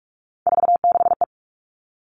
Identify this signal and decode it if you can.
Morse code, heard as a keyed tone, about 44 wpm, 700 Hz. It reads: 46E
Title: All Cut Numbers - 3-Digit 45wpm